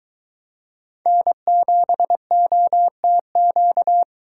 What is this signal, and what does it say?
Morse code, heard as a keyed tone, about 22 wpm, 700 Hz. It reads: N7OTQ